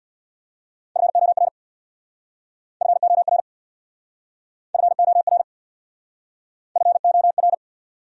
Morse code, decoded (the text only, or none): FOR FOR FOR FOR